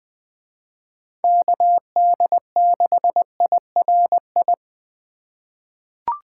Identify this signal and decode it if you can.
Morse code, heard as a keyed tone, about 20 wpm, 700 Hz. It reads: KD6IRI
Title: K1USN SST QSO Practice - 20wpm